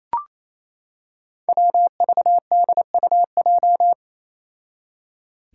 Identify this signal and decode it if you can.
Morse code, heard as a keyed tone, about 27 wpm, 700 Hz. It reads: WVDUJ
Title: Five Challenging Letter-Number Combinations 28wpm